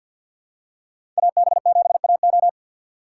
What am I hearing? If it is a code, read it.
Morse code, heard as a keyed tone, about 50 wpm, 700 Hz. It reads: AB7AO
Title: USA Call Signs - 5 Characters - 2x Character Spacing - 50wpm